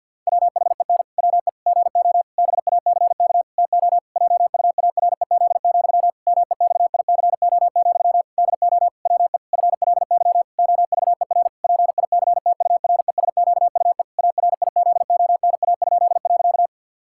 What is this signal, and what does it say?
Morse code, heard as a keyed tone, about 50 wpm, 700 Hz. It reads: WHEN WE GO BACK TO JUAREZ, MEXICO, DO WE FLY OVER PICTURESQUE ARIZONA?.
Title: Pangram Challenge v2 - 50wpm